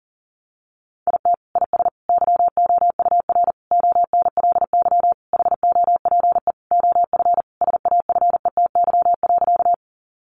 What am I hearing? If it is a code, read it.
Morse code, heard as a keyed tone, about 40 wpm, 700 Hz. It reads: IT IS YOUR ONLY HOPE OF SAFETY.
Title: Sentences from Top 1000 Words 40wpm